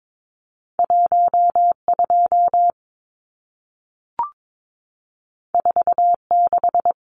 Morse code, heard as a keyed tone, about 22 wpm, 700 Hz. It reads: 12 46
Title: Numbers - 2-Digit 22wpm